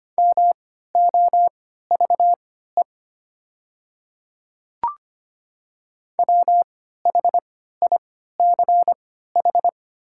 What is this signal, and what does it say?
Morse code, heard as a keyed tone, about 25 wpm, spaced as for 14 wpm, 690 Hz. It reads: MOVE WHICH
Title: 3x Character Spacing - Top 300 Words - 25wpm